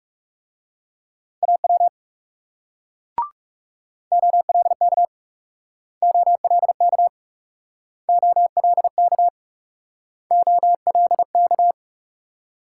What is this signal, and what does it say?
Morse code, heard as a keyed tone, about 40 wpm, 700 Hz. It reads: AW OLK OLK OLK OLK